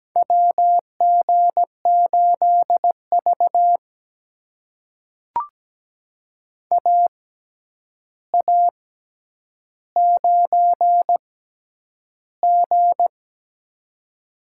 Morse code, heard as a keyed tone, about 17 wpm, spaced as for 9 wpm, 700 Hz. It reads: WG8V AA9G
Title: USA Call Signs - 4 Characters - 6x Character Spacing - 17wpm